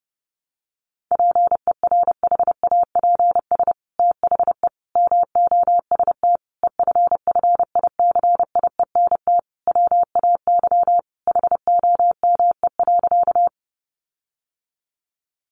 Morse code, heard as a keyed tone, about 28 wpm, 700 Hz. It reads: PERHAPS THE MOST EFFICIENT WAY HOME.
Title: Sentences from Top 1300 Words 30wpm